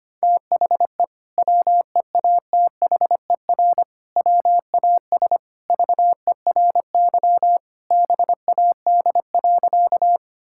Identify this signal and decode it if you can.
Morse code, heard as a keyed tone, about 25 wpm, 700 Hz. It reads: THE WEATHER WAS VERY BAD.
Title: Sentences from Top 1000 Words - Review Part 5 - 25wpm